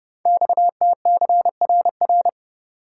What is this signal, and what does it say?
Morse code, heard as a keyed tone, about 30 wpm, 700 Hz. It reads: XTCRR
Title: Code Groups - 5 Letters - 30wpm